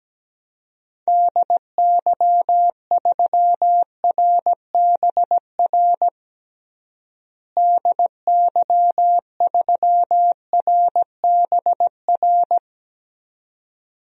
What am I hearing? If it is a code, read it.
Morse code, heard as a keyed tone, about 17 wpm, 700 Hz. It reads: DY3RBR DY3RBR
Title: DXCC-100 Call Signs - 6 Characters - Repeated 2x - 17wpm